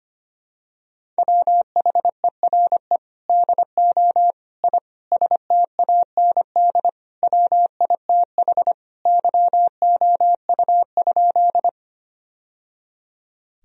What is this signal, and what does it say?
Morse code, heard as a keyed tone, about 25 wpm, 700 Hz. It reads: WHERE DO I STAND WITH YOU?
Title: Sentences from Top 1000 Words - Review Part 1 - 25wpm